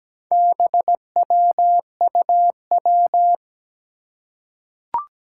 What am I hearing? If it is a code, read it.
Morse code, heard as a keyed tone, about 17 wpm, 700 Hz. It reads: BWUW
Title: Four Challenging Letter-Number Combinations 17wpm